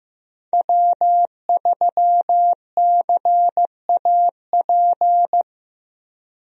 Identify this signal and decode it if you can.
Morse code, heard as a keyed tone, about 15 wpm, 700 Hz. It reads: W3CAP
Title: USA Call Signs - 5 Characters - Repeated 3x - 15wpm